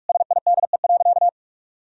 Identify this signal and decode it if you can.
Morse code, heard as a keyed tone, about 46 wpm, 700 Hz. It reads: SIDE.
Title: Sentences from Top 1000 Words - Review Part 1 - 45wpm